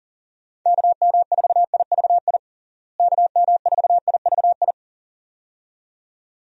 Morse code, heard as a keyed tone, about 40 wpm, 700 Hz. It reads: KM4IVI KM4IVI
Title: USA Call Signs - 6 Characters - Repeated 2x - 40wpm